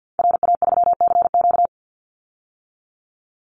Morse code, heard as a keyed tone, about 50 wpm, 700 Hz. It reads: RA3CQ
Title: Call Signs - 50wpm